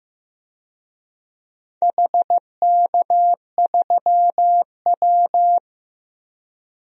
Morse code, heard as a keyed tone, about 15 wpm, 700 Hz. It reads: HK3W
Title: DXCC-100 Call Signs - 4 Characters - Repeated 3x - 15wpm